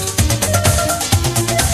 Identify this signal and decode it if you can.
There is no code.